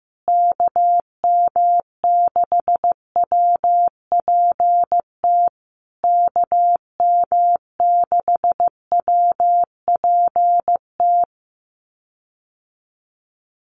Morse code, heard as a keyed tone, about 15 wpm, 700 Hz. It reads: KM6WPT KM6WPT
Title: USA Call Signs Repeated 3x 15wpm